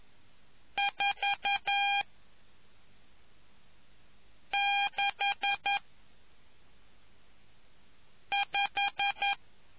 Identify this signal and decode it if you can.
Morse code, heard as a keyed tone, about 11 wpm, 800 Hz. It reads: 4 6 5